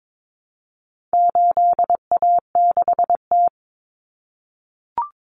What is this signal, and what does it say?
Morse code, heard as a keyed tone, about 22 wpm, 700 Hz. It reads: 8A6T
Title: Common Cut Numbers - 4-Digit 22wpm